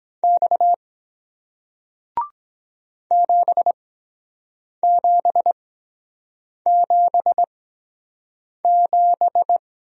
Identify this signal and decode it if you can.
Morse code, heard as a keyed tone, about 23 wpm, spaced as for 7 wpm, 700 Hz. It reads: X 7777